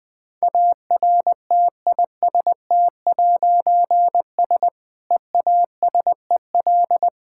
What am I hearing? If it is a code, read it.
Morse code, heard as a keyed tone, about 20 wpm, 700 Hz. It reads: ARTIST'S EASEL
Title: Difficult Words - Speed-Racing with Hints - 20wpm